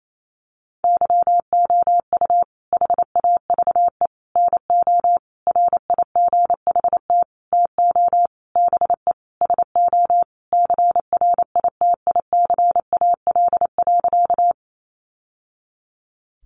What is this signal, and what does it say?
Morse code, heard as a keyed tone, about 28 wpm, 700 Hz. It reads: YOU HAVE NO RIGHT TO BE SO CRITICAL.
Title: Sentences from Top 1100 Words 28wpm